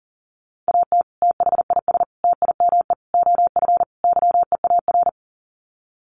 Morse code, heard as a keyed tone, about 40 wpm, 700 Hz. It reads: AT THIS TIME OF YEAR
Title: Sets of 5 Words Encore2 40wpm